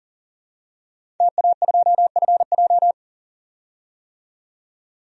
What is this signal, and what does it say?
Morse code, heard as a keyed tone, about 40 wpm, 700 Hz. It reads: TA2FJ
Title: DXCC-100 Call Signs - 5 Characters - 40wpm